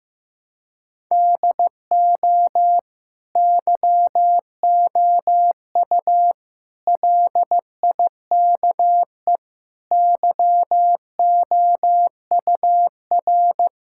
Morse code, heard as a keyed tone, about 15 wpm, 700 Hz. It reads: DO YOU LIKE YOUR
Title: Sentences from Top 1000 Words - Review Part 6 - 15wpm